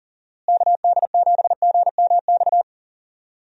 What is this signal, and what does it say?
Morse code, heard as a keyed tone, about 40 wpm, 700 Hz. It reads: KD7GMX
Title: USA Call Signs - 6 Characters - Repeated 3x - 40wpm